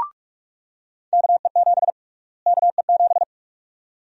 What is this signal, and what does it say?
Morse code, heard as a keyed tone, about 44 wpm, 700 Hz. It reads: KE7 KE7
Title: USA Call Sign Prefixes - Repeated 2x - 45wpm